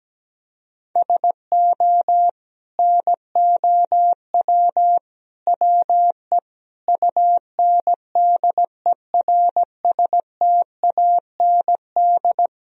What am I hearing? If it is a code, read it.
Morse code, heard as a keyed tone, about 17 wpm, 700 Hz. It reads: SO NOW WE UNDERSTAND
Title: Sentences from Top 1000 Words - Review Part 1 - 17wpm